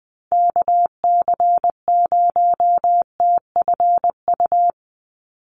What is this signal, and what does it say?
Morse code, heard as a keyed tone, about 20 wpm, 700 Hz. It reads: KC0TFU
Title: USA Call Signs 20wpm